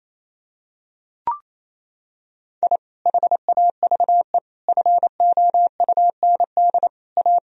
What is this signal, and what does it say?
Morse code, heard as a keyed tone, about 27 wpm, 700 Hz. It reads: I HAVE FOUND A
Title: Sentences from Top 1000 Words - Review Part 3 - 28wpm